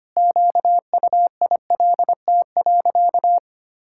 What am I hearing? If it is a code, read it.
Morse code, heard as a keyed tone, about 25 wpm, 700 Hz. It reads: QUILT.
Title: Pangram Challenge v2 - 25wpm